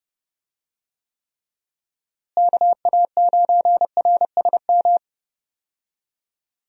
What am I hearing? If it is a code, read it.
Morse code, heard as a keyed tone, about 30 wpm, 700 Hz. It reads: KA9RSM
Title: USA Call Signs - 6 Characters - Repeated 3x - 30wpm